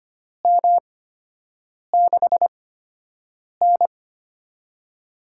Morse code, heard as a keyed tone, about 25 wpm, spaced as for 7 wpm, 700 Hz. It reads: M6N
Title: USA Call Signs - 8x Character Spacing - 25wpm